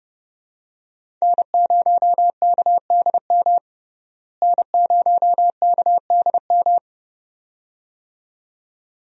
Morse code, heard as a keyed tone, about 31 wpm, 700 Hz. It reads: N0KDM N0KDM
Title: USA Call Signs - 5 Characters - Repeated 2x - 30wpm